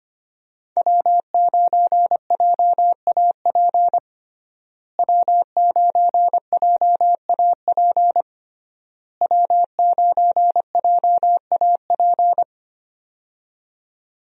Morse code, heard as a keyed tone, about 25 wpm, 700 Hz. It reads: W9JAP W9JAP W9JAP
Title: USA Call Signs - 5 Characters - Repeated 3x - 25wpm